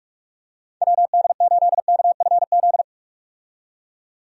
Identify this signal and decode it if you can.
Morse code, heard as a keyed tone, about 44 wpm, 700 Hz. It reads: WD8KFZ
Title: USA Call Signs - 6 Characters - 45wpm